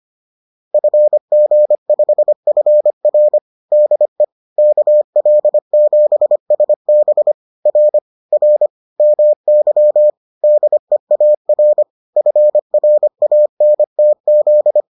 Morse code, heard as a keyed tone, about 25 wpm, 600 Hz. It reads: FG5FR DE KL7SB R R MY DEAR FRANTZ